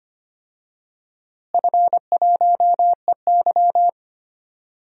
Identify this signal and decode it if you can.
Morse code, heard as a keyed tone, about 25 wpm, 700 Hz. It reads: F1EY
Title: DXCC-100 Call Signs - 4 Characters - Repeated 3x - 25wpm